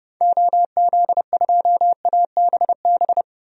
Morse code, heard as a keyed tone, about 30 wpm, 700 Hz. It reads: OZ2ABB
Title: DXCC-100 Call Signs - 6 Characters - Repeated 3x - 30wpm